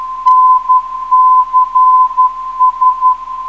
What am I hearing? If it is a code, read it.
Morse code, heard as a keyed tone, about 11 wpm, 1020 Hz. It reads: NCS
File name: ndb zvok / NCS 409kHz